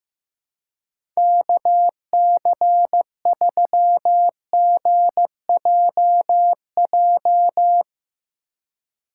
Morse code, heard as a keyed tone, about 15 wpm, 700 Hz. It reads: KC3GJJ
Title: USA Call Signs - 4x Character Spacing - 15wpm